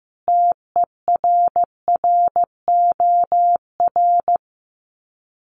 Morse code, heard as a keyed tone, about 15 wpm, 700 Hz. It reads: TERROR
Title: Sets of 5 Words Encore2 15wpm